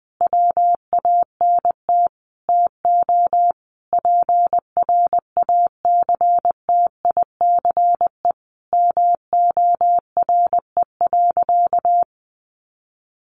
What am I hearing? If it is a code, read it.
Morse code, heard as a keyed tone, about 19 wpm, 700 Hz. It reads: WANT TO PRACTICE MORE.
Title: Sentences from Top 700 Words 20wpm